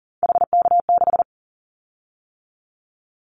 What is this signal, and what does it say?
Morse code, heard as a keyed tone, about 40 wpm, 700 Hz. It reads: HK6
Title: Callsign Prefixes for Top 100 DXCC - 40wpm